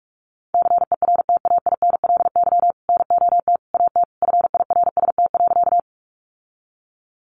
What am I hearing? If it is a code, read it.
Morse code, heard as a keyed tone, about 44 wpm, 700 Hz. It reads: CERTAINLY NOT AT FIRST.
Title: Sentences from Top 900 Words 45wpm